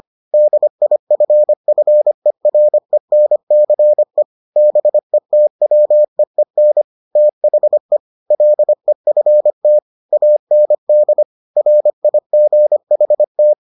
Morse code, heard as a keyed tone, about 25 wpm, 600 Hz. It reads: DIFFERENCE BETWEEN THE LEFT AND RIGHT